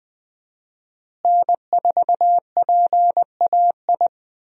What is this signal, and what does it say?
Morse code, heard as a keyed tone, about 20 wpm, 700 Hz. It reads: N4PAI
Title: USA Call Signs - 5 Characters - Repeated 2x - 20wpm